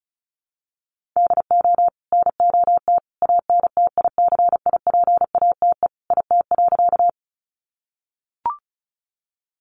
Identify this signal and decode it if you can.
Morse code, heard as a keyed tone, about 34 wpm, 700 Hz. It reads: DO NOT ANTICIPATE IT.
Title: Sentences from Top 1500 Words 35wpm